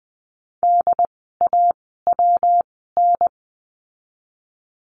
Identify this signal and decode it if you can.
Morse code, heard as a keyed tone, about 20 wpm, spaced as for 14 wpm, 700 Hz. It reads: DAWN
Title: 115 Lesson 19f - Letter w - Words 20wpm